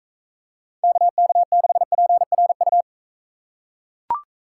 Morse code, heard as a keyed tone, about 40 wpm, 700 Hz. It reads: KK6PRU